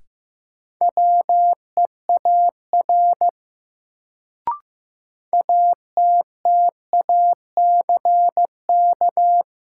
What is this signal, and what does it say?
Morse code, heard as a keyed tone, about 15 wpm, 700 Hz. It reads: WEAR ATTACK
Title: Sentences from Top 1500 Words - New Word Drill Alt - 15wpm